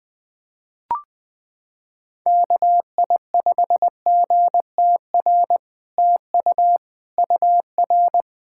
Morse code, heard as a keyed tone, about 20 wpm, 700 Hz. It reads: KI5GTR TU UR
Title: POTA QSO Practice - 20wpm